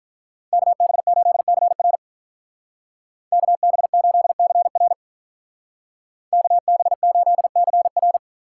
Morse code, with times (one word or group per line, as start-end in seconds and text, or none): 0.5-2.0 KB8CR
3.3-4.9 KB8CR
6.3-8.2 KB8CR